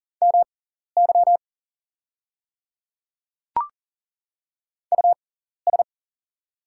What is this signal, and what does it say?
Morse code, heard as a keyed tone, about 41 wpm, spaced as for 14 wpm, 690 Hz. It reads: MY US
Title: 6x Character Spacing - Top 500 Words - 40wpm